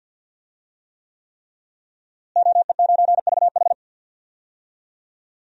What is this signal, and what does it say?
Morse code, heard as a keyed tone, about 53 wpm, 700 Hz. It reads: OE9VH
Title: DXCC-100 Call Signs - 5 Characters - Repeated 3x - 50wpm